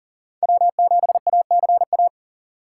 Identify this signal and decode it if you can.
Morse code, heard as a keyed tone, about 40 wpm, 700 Hz. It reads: W7ACA
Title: USA Call Signs - 5 Characters - 4x Character Spacing - 40wpm